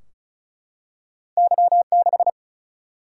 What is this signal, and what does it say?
Morse code, heard as a keyed tone, about 36 wpm, 700 Hz. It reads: Y6
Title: Single-Letter Single-Number 35wpm